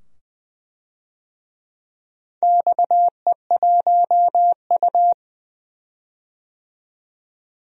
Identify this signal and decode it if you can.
Morse code, heard as a keyed tone, about 20 wpm, 700 Hz. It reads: XE1U
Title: DXCC-100 Call Signs - 4 Characters - 20wpm